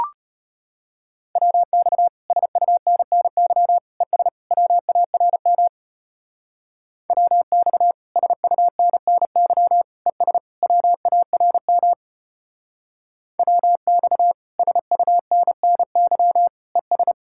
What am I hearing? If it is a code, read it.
Morse code, heard as a keyed tone, about 30 wpm, 700 Hz. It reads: WX SUNNY ES WARM WX SUNNY ES WARM WX SUNNY ES
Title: Speed-Racing Common QSO Phrases - 25wpm